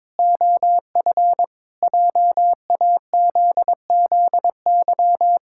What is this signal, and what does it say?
Morse code, heard as a keyed tone, about 22 wpm, 700 Hz. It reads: OF JAZZY
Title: Pangram Challenge v2 - 22wpm